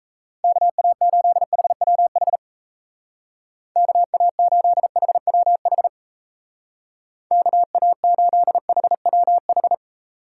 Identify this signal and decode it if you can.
Morse code, heard as a keyed tone, about 34 wpm, 700 Hz. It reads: KA8HWH KA8HWH KA8HWH
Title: Speed Racing - USA 2x3 Call Signs 28wpm